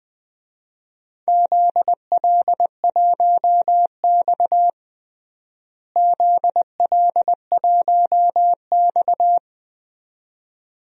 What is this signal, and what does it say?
Morse code, heard as a keyed tone, about 20 wpm, 700 Hz. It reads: ZL1X ZL1X
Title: DXCC-100 Call Signs - 4 Characters - Repeated 2x - 20wpm